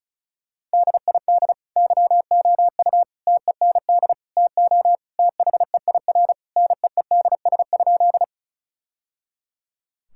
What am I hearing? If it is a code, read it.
Morse code, heard as a keyed tone, about 36 wpm, 700 Hz. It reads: DID YOU TEND TO THEIR NEEDS?
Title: Sentences from Top 1000 Words - Review Part 5 - 35wpm